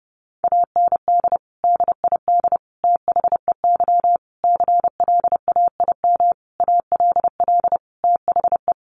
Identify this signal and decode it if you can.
Morse code, heard as a keyed tone, about 28 wpm, 700 Hz. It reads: AND DID THEY CLAIM ALL THE
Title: Sentences from Top 1500 Words 30wpm